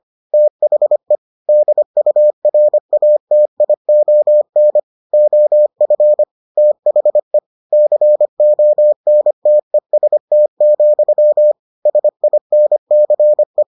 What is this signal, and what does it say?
Morse code, heard as a keyed tone, about 25 wpm, 600 Hz. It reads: THE DURATION OF THE CONTEST, SINCE